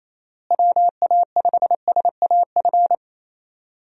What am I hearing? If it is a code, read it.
Morse code, heard as a keyed tone, about 27 wpm, 700 Hz. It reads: WA5SAF
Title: USA Call Signs - 6 Characters - Repeated 2x - 28wpm